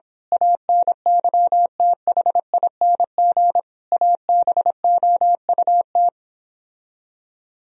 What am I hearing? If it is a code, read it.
Morse code, heard as a keyed tone, about 26 wpm, 700 Hz. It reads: ANYTHING ABOUT
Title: Speed-Racing - Sets of 5 Words 22wpm